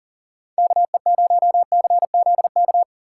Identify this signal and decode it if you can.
Morse code, heard as a keyed tone, about 50 wpm, 700 Hz. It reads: KE0CZK
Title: Speed Racing - USA 2x3 Call Signs 40wpm